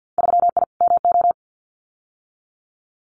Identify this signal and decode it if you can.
Morse code, heard as a keyed tone, about 50 wpm, 700 Hz. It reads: 3I MO
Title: Winter Field Day Exchanges 50wpm